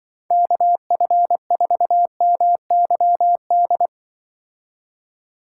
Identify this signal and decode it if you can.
Morse code, heard as a keyed tone, about 24 wpm, 700 Hz. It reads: KF4MYD
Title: Speed Racing - USA 2x3 Call Signs 20wpm